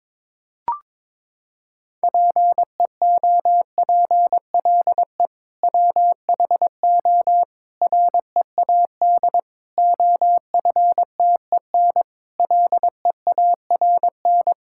Morse code, heard as a keyed tone, about 22 wpm, 700 Hz. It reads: PEOPLE WHO READ OFTEN LEARN
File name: practice / 22wpm / Sentences with 2-Word Phrases - 22wpm